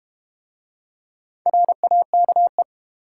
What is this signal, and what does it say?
Morse code, heard as a keyed tone, about 32 wpm, 700 Hz. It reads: RAKE